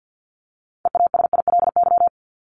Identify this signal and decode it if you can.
Morse code, heard as a keyed tone, about 50 wpm, 700 Hz. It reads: EASILY
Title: Top 751-1000 Words 50wpm